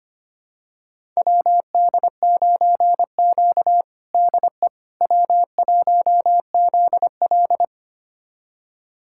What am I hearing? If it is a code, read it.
Morse code, heard as a keyed tone, about 25 wpm, 700 Hz. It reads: WD9Q DE W1ZL
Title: USA Callsign DE Callsign - 4 Characters - 25wpm